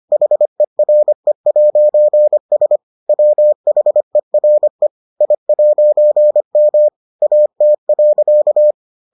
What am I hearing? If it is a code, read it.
Morse code, heard as a keyed tone, about 25 wpm, 600 Hz. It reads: HERE'S WHERE I'M AT.